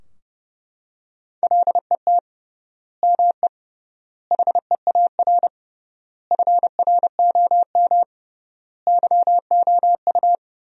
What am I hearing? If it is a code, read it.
Morse code, heard as a keyed tone, about 32 wpm, 700 Hz. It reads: LET ME HEAR FROM YOU
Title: 3x Word-Spacing - Sentences from Top 400 Words - 30wpm